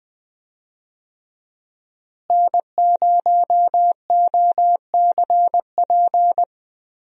Morse code, heard as a keyed tone, about 21 wpm, 700 Hz. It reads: N0OCP
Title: USA Call Signs - 5 Characters - Repeated 3x - 20wpm